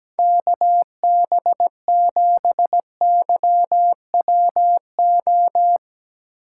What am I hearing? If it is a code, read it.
Morse code, heard as a keyed tone, about 17 wpm, 700 Hz. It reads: KB7YWO
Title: USA Call Signs - 8x Character Spacing - 17wpm